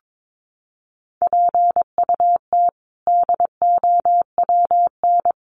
Morse code, heard as a keyed tone, about 22 wpm, 700 Hz. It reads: PUT DOWN
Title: Sentences from Top 400 Words 22wpm